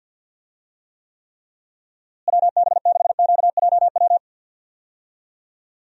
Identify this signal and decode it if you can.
Morse code, heard as a keyed tone, about 50 wpm, 700 Hz. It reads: WB6QJW
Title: USA Call Signs - 6 Characters - Repeated 3x - 50wpm